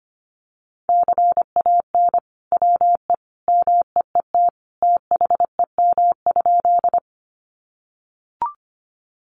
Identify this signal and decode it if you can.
Morse code, heard as a keyed tone, about 25 wpm, 700 Hz. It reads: CAN WE MEET THEM?
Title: Sentences from Top 1100 Words 25wpm